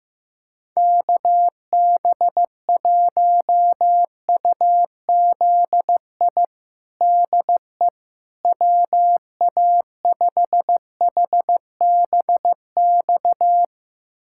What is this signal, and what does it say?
Morse code, heard as a keyed tone, about 16 wpm, 700 Hz. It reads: KB1UZI DE WA5HBX
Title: USA Callsign DE Callsign 15wpm